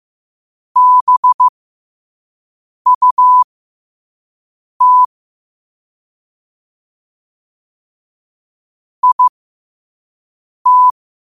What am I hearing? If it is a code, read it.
Morse code, heard as a keyed tone, about 15 wpm, spaced as for 5 wpm, 1000 Hz. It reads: BUT IT
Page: New quotes every day in morse code at 5 Words per minute.